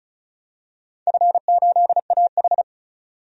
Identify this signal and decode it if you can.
Morse code, heard as a keyed tone, about 35 wpm, 700 Hz. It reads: F8AH
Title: DXCC-100 Call Signs - 4 Characters - 35wpm